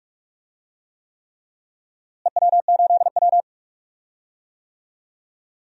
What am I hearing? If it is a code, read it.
Morse code, heard as a keyed tone, about 43 wpm, 700 Hz. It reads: EW8W
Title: DXCC-100 Call Signs - 4 Characters - Repeated 3x - 45wpm